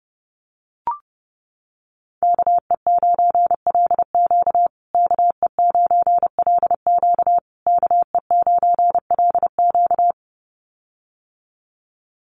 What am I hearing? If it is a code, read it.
Morse code, heard as a keyed tone, about 30 wpm, 700 Hz. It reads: KE9LQ KE9LQ KE9LQ
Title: USA Call Signs Repeated 3x 30wpm